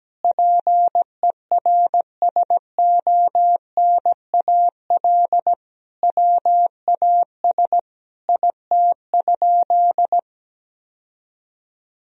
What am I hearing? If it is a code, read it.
Morse code, heard as a keyed tone, about 17 wpm, 700 Hz. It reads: PERSONAL WAS IT?
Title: Sentences from Top 1000 Words - Review Part 4 - 17wpm